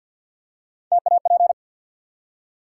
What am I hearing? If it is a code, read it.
Morse code, heard as a keyed tone, about 54 wpm, 700 Hz. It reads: TAP
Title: Three Letter Words - 50wpm